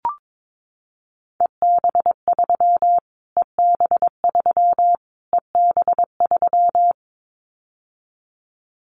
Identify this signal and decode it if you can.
Morse code, heard as a keyed tone, about 22 wpm, 700 Hz. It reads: EB3 EB3 EB3
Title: Callsign Prefixes for Top 25 DXCC - Repeated 3x - 22wpm